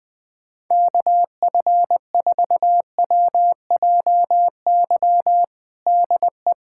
Morse code, heard as a keyed tone, about 20 wpm, 700 Hz. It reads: KF4WJY DE
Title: USA Callsign DE Callsign 20wpm